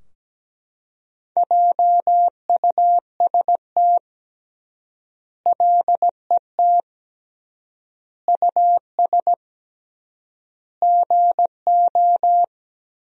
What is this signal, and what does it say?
Morse code, heard as a keyed tone, about 17 wpm, 700 Hz. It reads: JUST LET US GO
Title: 3x Word-Spacing - Sentences from Top 500 Words - 17wpm